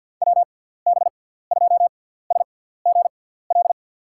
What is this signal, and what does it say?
Morse code, heard as a keyed tone, about 53 wpm, spaced as for 17 wpm, 700 Hz. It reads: WB2SGL
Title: USA Call Signs - 6x Character Spacing - 50wpm